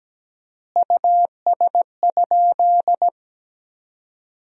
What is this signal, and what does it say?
Morse code, heard as a keyed tone, about 17 wpm, 700 Hz. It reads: US?
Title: 4x Word-Spacing - Sentences from Top 200 Words - 17wpm